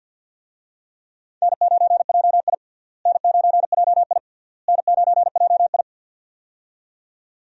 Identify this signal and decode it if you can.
Morse code, heard as a keyed tone, about 50 wpm, 700 Hz. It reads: N9JI N9JI N9JI
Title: USA Call Signs - 4 Characters - Repeated 3x - 50wpm